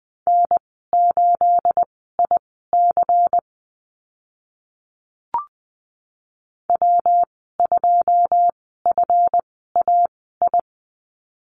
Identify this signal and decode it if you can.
Morse code, heard as a keyed tone, about 20 wpm, spaced as for 15 wpm, 700 Hz. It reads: N8IC W2FAI
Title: 213 Lesson 34g - Number 8 - Call Signs 20wpm